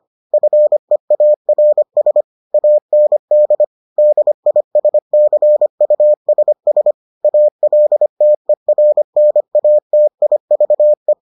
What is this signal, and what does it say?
Morse code, heard as a keyed tone, about 25 wpm, 600 Hz. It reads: FEARS AND DISCUSS ALTERNATIVE